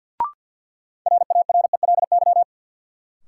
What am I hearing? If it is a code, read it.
Morse code, heard as a keyed tone, about 52 wpm, 700 Hz. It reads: RARELY
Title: Sentences from Top 1400 Words - New Word Drill - 50wpm